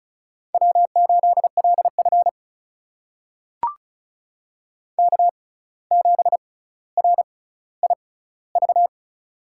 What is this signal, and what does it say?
Morse code, heard as a keyed tone, about 34 wpm, spaced as for 17 wpm, 700 Hz. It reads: W8LF K7RIV
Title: USA Call Signs - 6x Character Spacing - 35wpm